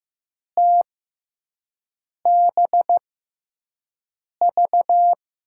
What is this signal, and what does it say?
Morse code, heard as a keyed tone, about 15 wpm, spaced as for 5 wpm, 700 Hz. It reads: TBV
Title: USA Call Signs - 5 Characters - 6x Character Spacing - 15wpm